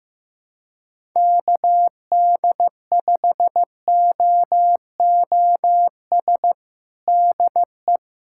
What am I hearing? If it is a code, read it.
Morse code, heard as a keyed tone, about 15 wpm, 700 Hz. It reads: KD5OOS DE
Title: USA Callsign DE Callsign 15wpm